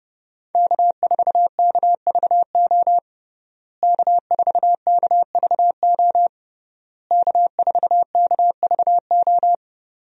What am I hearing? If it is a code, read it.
Morse code, heard as a keyed tone, about 30 wpm, 700 Hz. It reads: K4KVO K4KVO K4KVO
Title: USA Call Signs - 5 Characters - Repeated 3x - 30wpm